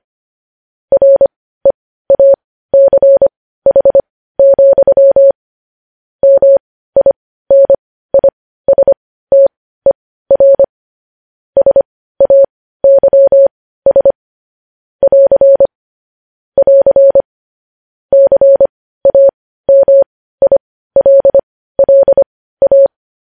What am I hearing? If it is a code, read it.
Morse code, heard as a keyed tone, about 25 wpm, spaced as for 15 wpm, 550 Hz. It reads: REACH, MINISTER SAYS <AR> <AR> CAMILLA
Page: news-headlines-in-morse-code-at-15-wpm